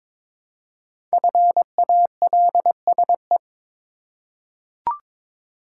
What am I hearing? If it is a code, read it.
Morse code, heard as a keyed tone, about 22 wpm, 700 Hz. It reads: FALSE
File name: 5 Letter Words - First 3 Letters Given - 22wpm